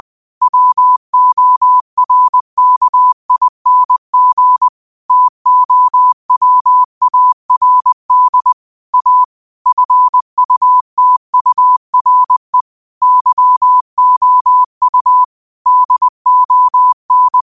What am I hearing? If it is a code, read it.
Morse code, heard as a keyed tone, about 20 wpm, 1000 Hz. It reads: WORKING TOWARD A FUTURE YOU DON